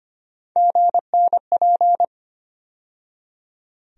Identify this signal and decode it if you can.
Morse code, heard as a keyed tone, about 25 wpm, 700 Hz. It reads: GNP
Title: USA Call Sign Postfixes - 3 Letters 25wpm